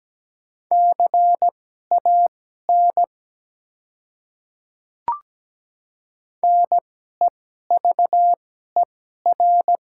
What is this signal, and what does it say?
Morse code, heard as a keyed tone, about 17 wpm, spaced as for 12 wpm, 690 Hz. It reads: CAN NEVER
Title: 2x Character Spacing - Top 300 Words - 17wpm